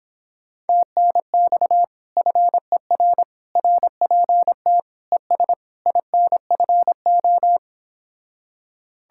Morse code, heard as a keyed tone, about 26 wpm, 700 Hz. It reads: TNX FER RPT ES INFO